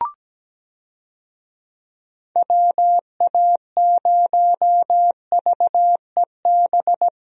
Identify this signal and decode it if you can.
Morse code, heard as a keyed tone, about 17 wpm, 700 Hz. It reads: WA0VEB
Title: USA Call Signs - 6 Characters - 17wpm